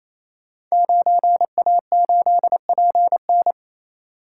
Morse code, heard as a keyed tone, about 27 wpm, 700 Hz. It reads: 9A8PN
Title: Code Groups - 5 Letter:Numbers - 28wpm